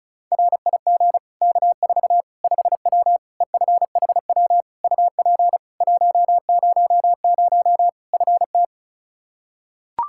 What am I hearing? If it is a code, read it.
Morse code, heard as a keyed tone, about 34 wpm, 700 Hz. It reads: RIG K4 5W EFHW UP 100 FT
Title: Practice QSO - Exchange 2 - Info - 35wpm